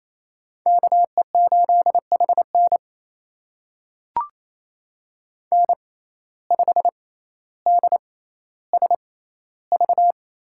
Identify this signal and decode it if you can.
Morse code, heard as a keyed tone, about 28 wpm, spaced as for 14 wpm, 700 Hz. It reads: KE8HN N5DSV
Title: USA Call Signs - 5 Characters - 6x Character Spacing - 28wpm